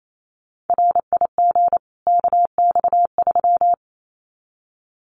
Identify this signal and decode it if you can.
Morse code, heard as a keyed tone, about 28 wpm, 700 Hz. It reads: RIG KX3
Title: Common QSO Phrases 28wpm